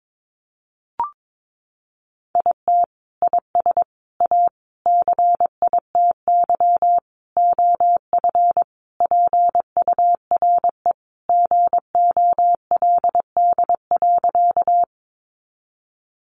Morse code, heard as a keyed tone, about 22 wpm, 700 Hz. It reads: IT IS A CITY OF PURE GOLD.
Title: Sentences from Top 1500 Words 22wpm